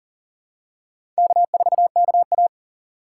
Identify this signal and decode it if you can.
Morse code, heard as a keyed tone, about 39 wpm, 700 Hz. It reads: K4KA
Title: USA Call Signs - 4 Characters - 6x Character Spacing - 40wpm